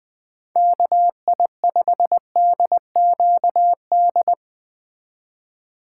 Morse code, heard as a keyed tone, about 20 wpm, 700 Hz. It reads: KI5DQD
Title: Speed Racing - USA Call Signs 17wpm